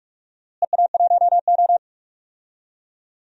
Morse code, heard as a keyed tone, about 44 wpm, 700 Hz. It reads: EA1O
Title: DXCC-100 Call Signs - 4 Characters - Repeated 2x - 45wpm